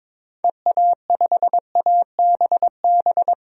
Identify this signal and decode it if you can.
Morse code, heard as a keyed tone, about 22 wpm, 700 Hz. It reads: EA5ABB
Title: DXCC-100 Call Signs - 6 Characters - 22wpm